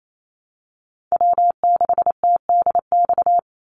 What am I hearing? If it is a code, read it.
Morse code, heard as a keyed tone, about 28 wpm, 700 Hz. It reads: W6TDX
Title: NA SOTA Chaser Callsigns 28wpm